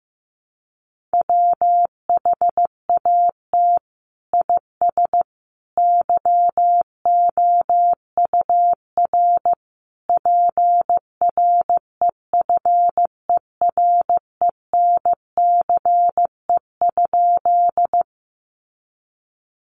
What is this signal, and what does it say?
Morse code, heard as a keyed tone, about 15 wpm, 700 Hz. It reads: WHAT IS YOUR PREFERENCE?
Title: Sentences from Top 1500 Words 15wpm